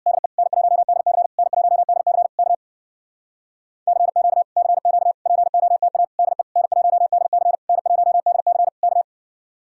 Morse code, heard as a keyed tone, about 63 wpm, 700 Hz. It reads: DE N1DC N1DC K CQ CQ POTA DE N1DC N1DC K